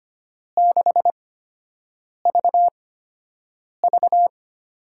Morse code, heard as a keyed tone, about 25 wpm, spaced as for 7 wpm, 700 Hz. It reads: 6VV
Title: USA Call Signs - 8x Character Spacing - 25wpm